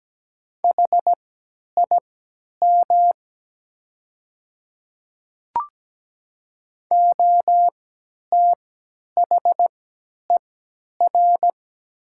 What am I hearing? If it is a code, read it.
Morse code, heard as a keyed tone, about 17 wpm, spaced as for 10 wpm, 690 Hz. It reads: HIM OTHER
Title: 3x Character Spacing - Top 200 Words - 17wpm